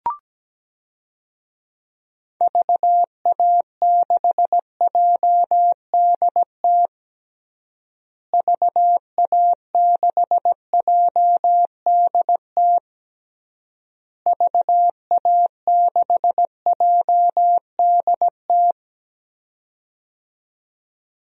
Morse code, heard as a keyed tone, about 17 wpm, 700 Hz. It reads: VA6JDT VA6JDT VA6JDT
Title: DXCC-100 Call Signs - 6 Characters - Repeated 3x - 17wpm